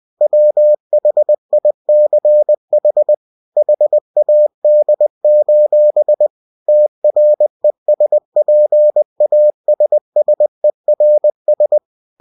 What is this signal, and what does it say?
Morse code, heard as a keyed tone, about 20 wpm, 600 Hz. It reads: WHICH HAD: TRESPASSERS
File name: cw-books / winnie-the-pooh / winnie-20wpm-page-0020